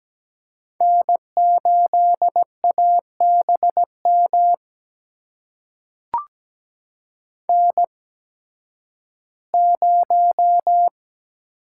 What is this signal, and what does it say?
Morse code, heard as a keyed tone, about 17 wpm, spaced as for 11 wpm, 700 Hz. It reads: N8ABM N0
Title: USA Call Signs - 5 Characters - 8x Character Spacing - 17wpm